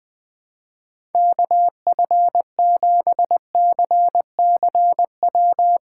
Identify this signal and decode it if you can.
Morse code, heard as a keyed tone, about 20 wpm, 700 Hz. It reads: KF7CCW
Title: USA Callsign DE Callsign - 6 Characters - 20wpm